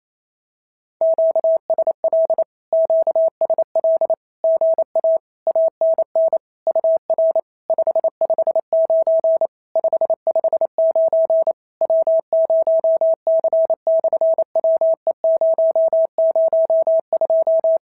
Morse code, heard as a keyed tone, about 28 wpm, 650 Hz. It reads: QSL QSL GA ANN UR 559 559 W0C/WE002
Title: SOTA QSO Practice 28wpm